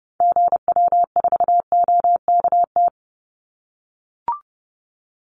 Morse code, heard as a keyed tone, about 30 wpm, 700 Hz. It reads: GW4OKT
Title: Call Signs - 30wpm